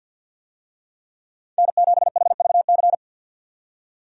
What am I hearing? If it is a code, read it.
Morse code, heard as a keyed tone, about 50 wpm, 700 Hz. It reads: N7HVC
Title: USA Call Signs - 5 Characters - 50wpm